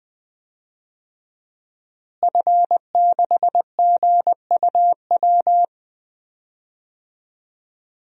Morse code, heard as a keyed tone, about 20 wpm, 700 Hz. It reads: F6GUW
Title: DXCC-100 Call Signs - 5 Characters - Repeated 2x - 20wpm